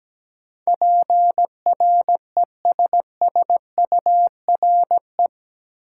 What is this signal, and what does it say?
Morse code, heard as a keyed tone, about 17 wpm, 700 Hz. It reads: PRESSURE
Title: Sentences from Top 1000 Words - Review Part 6 - New Word Drill - 17wpm